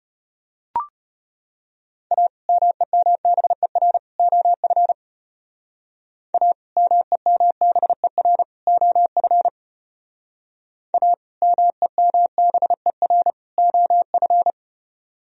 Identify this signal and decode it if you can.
Morse code, heard as a keyed tone, about 32 wpm, 700 Hz. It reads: A MEMBER OF A MEMBER OF A MEMBER OF